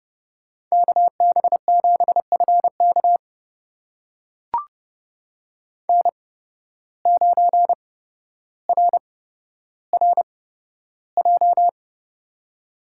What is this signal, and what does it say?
Morse code, heard as a keyed tone, about 31 wpm, spaced as for 13 wpm, 700 Hz. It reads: KB7FK N9RRJ
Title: USA Call Signs - 5 Characters - 8x Character Spacing - 30wpm